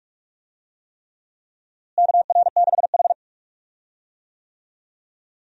Morse code, heard as a keyed tone, about 46 wpm, 700 Hz. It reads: KR6H
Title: USA Call Signs - 4 Characters - Repeated 3x - 45wpm